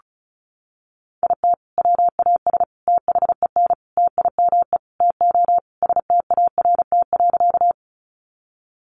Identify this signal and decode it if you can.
Morse code, heard as a keyed tone, about 34 wpm, 700 Hz. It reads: IT WAS THEN TIME TO START.
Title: Sentences from Top 300 Words 35wpm